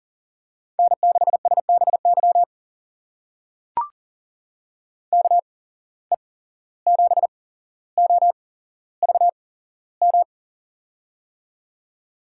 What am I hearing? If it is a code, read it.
Morse code, heard as a keyed tone, about 41 wpm, spaced as for 16 wpm, 690 Hz. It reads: N6SBY KE7OVM
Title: USA Call Signs - 8x Character Spacing - 40wpm